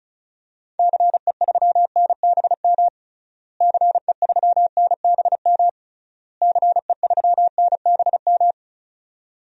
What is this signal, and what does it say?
Morse code, heard as a keyed tone, about 36 wpm, 700 Hz. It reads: CE3NBM CE3NBM CE3NBM
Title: DXCC-100 Call Signs - 6 Characters - Repeated 3x - 35wpm